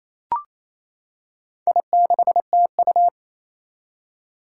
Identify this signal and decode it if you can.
Morse code, heard as a keyed tone, about 28 wpm, 700 Hz. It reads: I6TU
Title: Four Challenging Letter-Number Combinations 28wpm